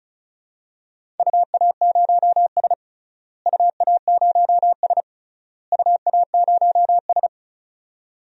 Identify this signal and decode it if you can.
Morse code, heard as a keyed tone, about 34 wpm, 700 Hz. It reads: UA0S UA0S UA0S
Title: DXCC-100 Call Signs - 4 Characters - Repeated 3x - 35wpm